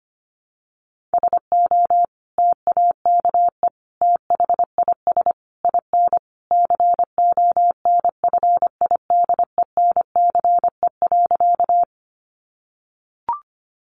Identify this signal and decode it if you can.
Morse code, heard as a keyed tone, about 25 wpm, 700 Hz. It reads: SO TAKE THIS IN CONFIDENCE.
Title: Sentences from Top 1500 Words 25wpm